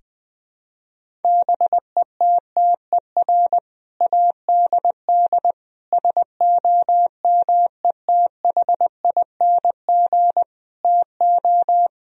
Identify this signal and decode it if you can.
Morse code, heard as a keyed tone, about 20 wpm, 700 Hz. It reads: BETTER ADD SOMETHING TO
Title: Sentences from Top 1000 Words - Review Part 1 - 20wpm